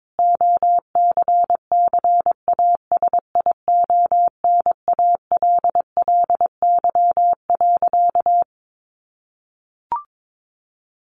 Morse code, heard as a keyed tone, about 22 wpm, 700 Hz. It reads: OCCASIONALLY.
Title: Sentences from Top 1300 Words 22wpm